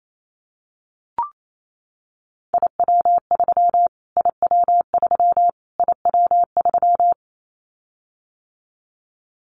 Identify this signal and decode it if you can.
Morse code, heard as a keyed tone, about 28 wpm, 700 Hz. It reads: IW3 IW3 IW3
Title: Callsign Prefixes for Top 25 DXCC - Repeated 3x - 28wpm